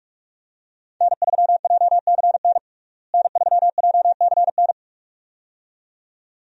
Morse code, heard as a keyed tone, about 46 wpm, 700 Hz. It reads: N3JCN N3JCN
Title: USA Call Signs - 5 Characters - Repeated 2x - 45wpm